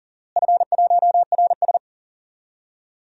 Morse code, heard as a keyed tone, about 39 wpm, 690 Hz. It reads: F1RS
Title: DXCC-100 Call Signs - 4 Characters - Repeated 2x - 40wpm